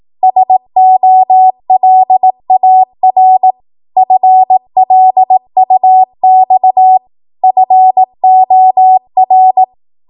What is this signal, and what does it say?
Morse code, heard as a keyed tone, about 18 wpm, 750 Hz. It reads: SOLAR FLUX FOR
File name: Morse / 18WPM